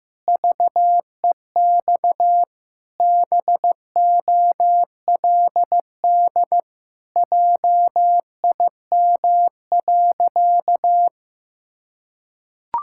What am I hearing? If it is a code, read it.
Morse code, heard as a keyed tone, about 15 wpm, 700 Hz. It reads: VEX BOLD JIM.
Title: Pangram Challenge v2 - 15wpm